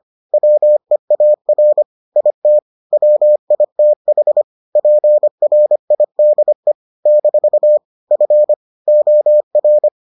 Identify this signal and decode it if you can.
Morse code, heard as a keyed tone, about 25 wpm, 600 Hz. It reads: WEAR IT WITH PRIDE - F OR